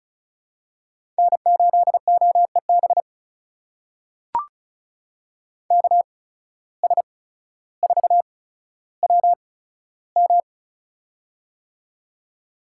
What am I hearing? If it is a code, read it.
Morse code, heard as a keyed tone, about 36 wpm, spaced as for 15 wpm, 700 Hz. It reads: N8OEB KS4WM
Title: USA Call Signs - 8x Character Spacing - 35wpm